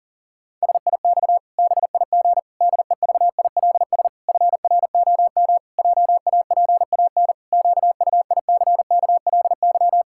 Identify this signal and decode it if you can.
Morse code, heard as a keyed tone, about 40 wpm, 700 Hz. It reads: SIX BIG DEVILS FROM JAPAN QUICKLY
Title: Pangram Challenge v2 - 40wpm